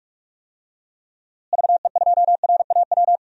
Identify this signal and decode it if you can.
Morse code, heard as a keyed tone, about 43 wpm, 700 Hz. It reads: VE2RAW
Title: DXCC-100 Call Signs - 6 Characters - Repeated 3x - 45wpm